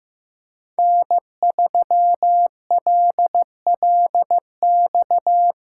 Morse code, heard as a keyed tone, about 15 wpm, 700 Hz. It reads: N3LLX
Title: USA Callsign DE Callsign 15wpm